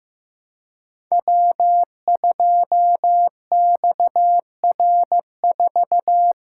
Morse code, heard as a keyed tone, about 15 wpm, 700 Hz. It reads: W2XR4
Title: Code Groups - 5 Letter:Numbers - 15wpm